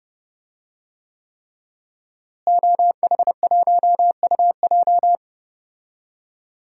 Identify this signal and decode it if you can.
Morse code, heard as a keyed tone, about 28 wpm, 700 Hz. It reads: OH1UJ
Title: DXCC-100 Call Signs - 5 Characters - Repeated 3x - 30wpm